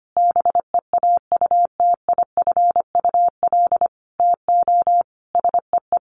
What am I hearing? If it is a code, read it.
Morse code, heard as a keyed tone, about 25 wpm, 700 Hz. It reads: BEAUTIFUL TO SEE
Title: Sentences from Top 900 Words 25wpm